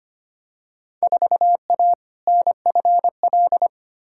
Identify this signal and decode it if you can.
Morse code, heard as a keyed tone, about 25 wpm, 700 Hz. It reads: 4A NFL
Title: Field Day Exchanges - 25wpm